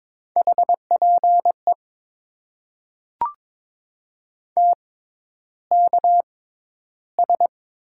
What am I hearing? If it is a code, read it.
Morse code, heard as a keyed tone, about 22 wpm, spaced as for 11 wpm, 700 Hz. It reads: HPE TKS
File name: Top 90 QSO Elements - 6x Character Spacing - 22wpm